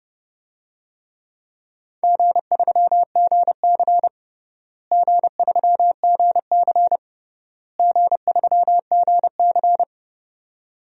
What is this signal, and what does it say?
Morse code, heard as a keyed tone, about 30 wpm, 700 Hz. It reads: G3GC G3GC G3GC
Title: DXCC-100 Call Signs - 4 Characters - Repeated 3x - 30wpm